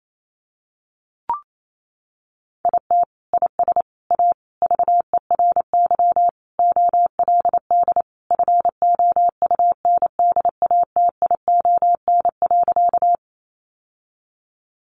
Morse code, heard as a keyed tone, about 28 wpm, 700 Hz. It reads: IT IS A VERY OLD FOUNDATION.
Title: Sentences from Top 1200 Words 28wpm